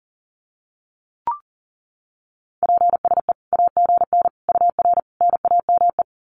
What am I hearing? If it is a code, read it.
Morse code, heard as a keyed tone, about 40 wpm, 700 Hz. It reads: PSE AGN UR NAME
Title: Common QSO Phrases 40wpm